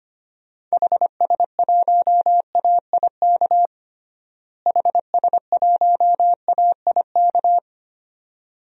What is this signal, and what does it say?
Morse code, heard as a keyed tone, about 25 wpm, 700 Hz. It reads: HS1AIK HS1AIK
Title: DXCC-100 Call Signs - 6 Characters - Repeated 2x - 25wpm